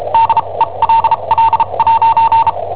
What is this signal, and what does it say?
Morse code, heard as a keyed tone, about 34 wpm, spaced as for 27 wpm, 910 Hz. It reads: DELL'
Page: Cassetta esercizi preparata da I7OHP per IW5ELP